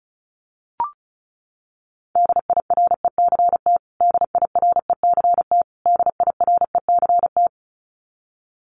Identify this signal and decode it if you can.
Morse code, heard as a keyed tone, about 34 wpm, 700 Hz. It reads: DIRECT DIRECT DIRECT
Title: Top 500 Words Repeated 35wpm